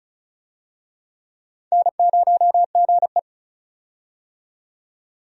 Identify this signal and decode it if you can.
Morse code, heard as a keyed tone, about 34 wpm, 700 Hz. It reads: N0GE
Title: USA Call Signs - 4 Characters - 35wpm